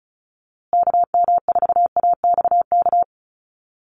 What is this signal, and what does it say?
Morse code, heard as a keyed tone, about 34 wpm, 700 Hz. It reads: KM4AXK
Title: USA Call Signs Repeated 3x 35wpm